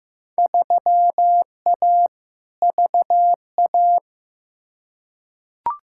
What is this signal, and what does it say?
Morse code, heard as a keyed tone, about 15 wpm, 700 Hz. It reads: 3A VA
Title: Field Day Exchanges - 15wpm